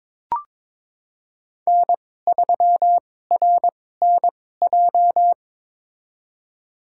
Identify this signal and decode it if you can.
Morse code, heard as a keyed tone, about 21 wpm, spaced as for 16 wpm, 700 Hz. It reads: N3RNJ
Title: USA Call Signs - 5 Characters - 2x Character Spacing - 22wpm